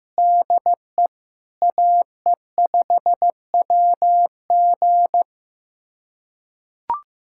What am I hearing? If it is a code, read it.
Morse code, heard as a keyed tone, about 15 wpm, 700 Hz. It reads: DE AE5WG
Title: USA Callsign DE Callsign - 5 Characters - 15wpm